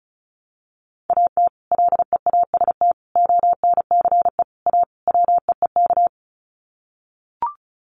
Morse code, heard as a keyed tone, about 34 wpm, 700 Hz. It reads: AT LEAST ONCE A WEEK
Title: Sets of 5 Words Encore2 35wpm